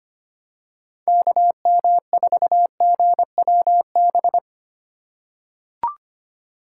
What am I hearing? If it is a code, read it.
Morse code, heard as a keyed tone, about 25 wpm, 700 Hz. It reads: KM4GWB
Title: USA Call Signs - 6 Characters - 4x Character Spacing - 25wpm